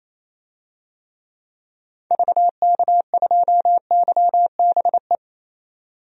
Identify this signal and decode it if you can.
Morse code, heard as a keyed tone, about 28 wpm, 700 Hz. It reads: VK2YBE
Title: DXCC-100 Call Signs - 6 Characters - 28wpm